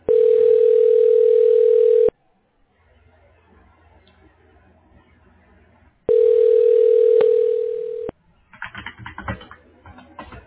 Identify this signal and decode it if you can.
Ringback tone (North America), 440 + 480 Hz.